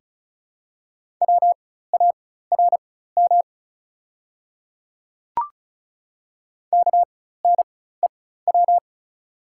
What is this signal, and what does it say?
Morse code, heard as a keyed tone, about 36 wpm, spaced as for 16 wpm, 690 Hz. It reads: WARM KNEW
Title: 4x Character Spacing - Top 500 Words - 35wpm